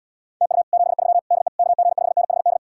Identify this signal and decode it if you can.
Morse code, heard as a keyed tone, about 77 wpm, 700 Hz. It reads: TU 73 DE KG5NHD